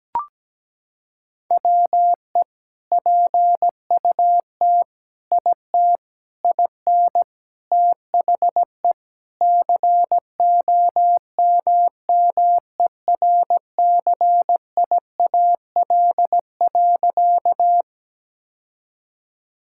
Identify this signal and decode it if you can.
Morse code, heard as a keyed tone, about 17 wpm, 700 Hz. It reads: WE PUT IT IN THE COMMERCIAL.
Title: Sentences from Top 1000 Words - Review Part 5 - 17wpm